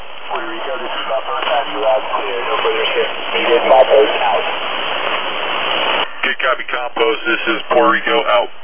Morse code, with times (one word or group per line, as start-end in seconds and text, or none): none